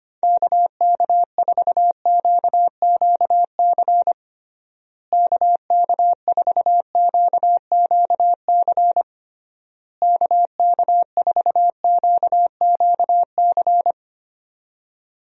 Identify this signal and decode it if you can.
Morse code, heard as a keyed tone, about 25 wpm, 700 Hz. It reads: KK4QQC KK4QQC KK4QQC
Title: USA Call Signs - 6 Characters - Repeated 3x - 25wpm